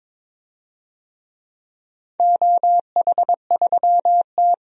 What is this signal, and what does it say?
Morse code, heard as a keyed tone, about 22 wpm, 700 Hz. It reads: OH3T
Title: DXCC-100 Call Signs - 4 Characters - 22wpm